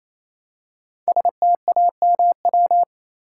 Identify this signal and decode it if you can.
Morse code, heard as a keyed tone, about 27 wpm, 700 Hz. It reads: STAMW
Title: Code Groups - 5 Letters - 28wpm